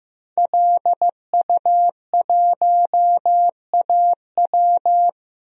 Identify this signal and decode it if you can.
Morse code, heard as a keyed tone, about 15 wpm, 700 Hz. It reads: LU1AW
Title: DXCC-100 Call Signs - 5 Characters - Repeated 2x - 15wpm